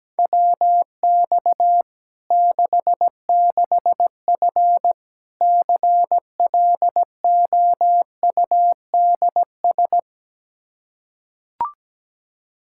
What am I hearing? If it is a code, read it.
Morse code, heard as a keyed tone, about 17 wpm, 700 Hz. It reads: WX 66F CLOUDS
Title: Practice QSO - Exchange 2 - Info - 17wpm